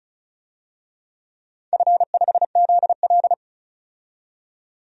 Morse code, heard as a keyed tone, about 36 wpm, 700 Hz. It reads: F5ZL
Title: DXCC-100 Call Signs - 4 Characters - 35wpm